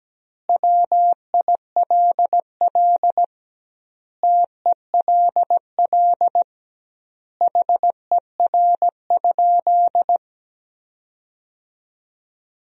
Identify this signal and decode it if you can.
Morse code, heard as a keyed tone, about 17 wpm, 700 Hz. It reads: WILL TELL HER?
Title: 2x Word-Spacing - Sentences from Top 200 Words - 17wpm